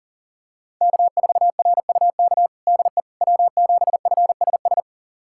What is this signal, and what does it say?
Morse code, heard as a keyed tone, about 40 wpm, 700 Hz. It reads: K4RUK DE W7FSS
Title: USA Callsign DE Callsign - 5 Characters - 40wpm